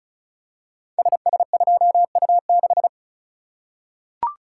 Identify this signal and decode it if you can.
Morse code, heard as a keyed tone, about 36 wpm, 700 Hz. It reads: SS2U6
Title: Code Groups - 5 Letter:Numbers - 35wpm